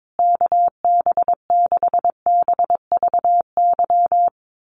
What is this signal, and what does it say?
Morse code, heard as a keyed tone, about 22 wpm, 700 Hz. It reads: KB6BVY
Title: USA Call Signs Repeated 2x 22wpm